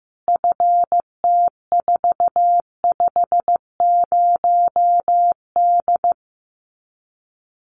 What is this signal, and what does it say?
Morse code, heard as a keyed tone, about 15 wpm, 700 Hz. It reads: FT450D
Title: Common QSO Phrases 15wpm